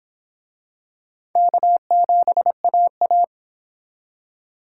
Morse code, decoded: K7AA

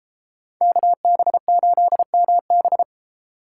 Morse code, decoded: KB8MB

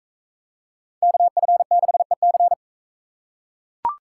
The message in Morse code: KF6EC

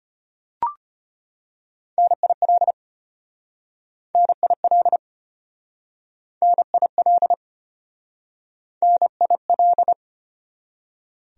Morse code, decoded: NIL NIL NIL NIL

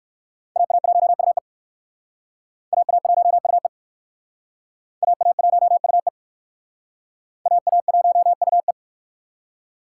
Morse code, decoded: AA1UE AA1UE AA1UE AA1UE

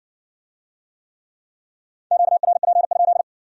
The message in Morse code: QRP?

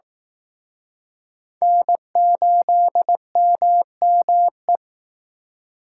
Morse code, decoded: N8MME

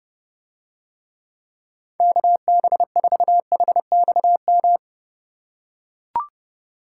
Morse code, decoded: KB4HXM

